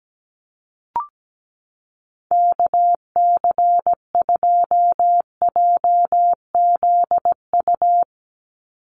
Morse code, heard as a keyed tone, about 17 wpm, 700 Hz. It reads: KC2JZU
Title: USA Call Signs 17wpm